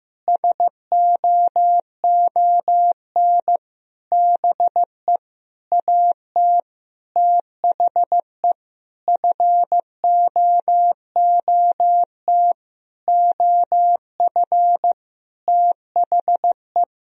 Morse code, heard as a keyed tone, about 15 wpm, 700 Hz. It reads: SOON BE AT THE FOOT OF THE
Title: Sentences from Top 1000 Words - Review Part 4 - 15wpm